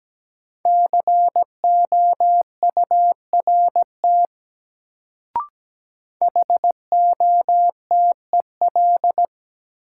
Morse code, heard as a keyed tone, about 17 wpm, 700 Hz. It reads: COURT HOTEL
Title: Sentences from Top 1600 Words - New Word Drill Alt - 17wpm